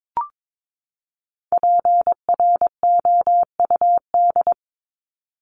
Morse code, heard as a keyed tone, about 22 wpm, 700 Hz. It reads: PROUD
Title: Top 1501-1750 Words 22wpm